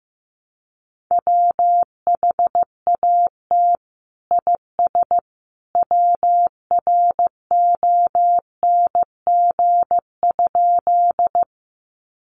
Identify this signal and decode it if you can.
Morse code, heard as a keyed tone, about 14 wpm, 700 Hz. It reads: WHAT IS WRONG?
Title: Sentences from Top 900 Words 15wpm